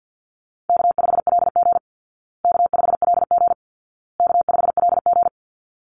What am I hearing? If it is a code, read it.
Morse code, heard as a keyed tone, about 50 wpm, 700 Hz. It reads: K5LG K5LG K5LG
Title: Easy Call Signs Repeated 3x - 50wpm